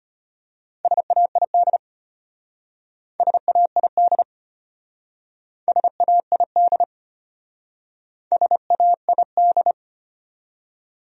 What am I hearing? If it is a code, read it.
Morse code, heard as a keyed tone, about 32 wpm, 700 Hz. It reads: SAID SAID SAID SAID